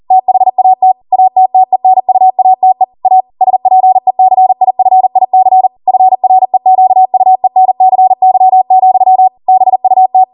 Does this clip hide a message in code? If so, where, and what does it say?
Yes: Morse code from 0.1 to 10.2 s, heard as a keyed tone, about 40 wpm, 750 Hz. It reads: THAT ATTENUATE A SPECIFIC FREQUENCY, BUT